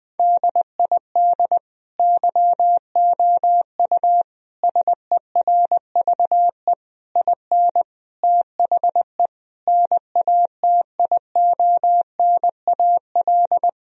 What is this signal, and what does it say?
Morse code, heard as a keyed tone, about 20 wpm, 700 Hz. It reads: DID YOU SERVE IN THE NATIONAL
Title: Sentences from Top 1000 Words - Review Part 4 - 20wpm